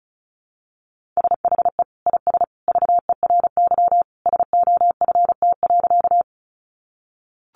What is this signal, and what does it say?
Morse code, heard as a keyed tone, about 34 wpm, 700 Hz. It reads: SHE IS VERY SOFT.
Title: Easy Phrases 35wpm